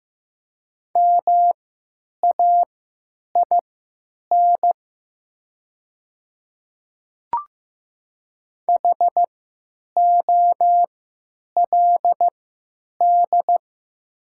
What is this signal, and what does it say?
Morse code, heard as a keyed tone, about 15 wpm, spaced as for 9 wpm, 700 Hz. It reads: MAIN HOLD
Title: 3x Character Spacing - Top 500 Words - 15wpm